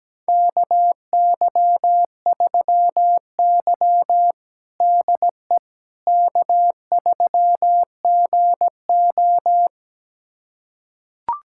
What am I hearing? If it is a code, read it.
Morse code, heard as a keyed tone, about 17 wpm, 700 Hz. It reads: KY3Y DE K3GO
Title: USA Callsign DE Callsign - 4 Characters - 17wpm